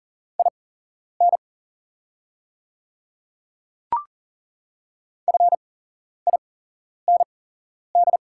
Morse code, heard as a keyed tone, about 40 wpm, spaced as for 11 wpm, 690 Hz. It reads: IN FIND
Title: 8x Character Spacing - Top 200 Words - 40wpm